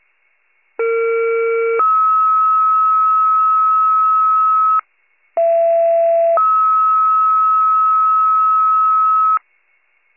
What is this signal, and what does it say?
Two-tone page, heard as a one-second tone, then a three-second tone, about 460 then 1290 Hz, about 670 then 1290 Hz. It is heard from 0.8 to 9.4 s.